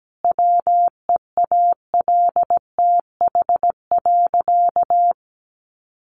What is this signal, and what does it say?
Morse code, heard as a keyed tone, about 17 wpm, 700 Hz. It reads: WEALTH.
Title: Sentences from Top 1200 Words 17wpm